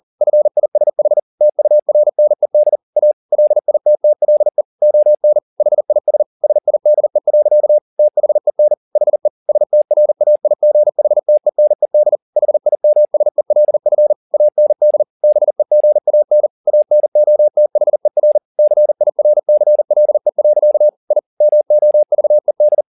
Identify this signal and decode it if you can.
Morse code, heard as a keyed tone, about 40 wpm, 600 Hz. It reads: FISH TURNED A LITTLE ON HIS SIDE. THEN HE STRAIGHTENED HIMSELF AND BEGAN ANOTHER CIRCLE. I MOVED
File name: the old man and the sea 2 40wpm